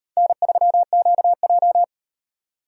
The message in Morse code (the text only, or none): N3QJ